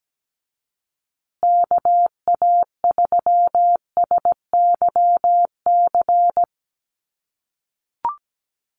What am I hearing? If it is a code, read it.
Morse code, heard as a keyed tone, about 17 wpm, 700 Hz. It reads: KA3SYC
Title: USA Call Signs Repeated 2x 17wpm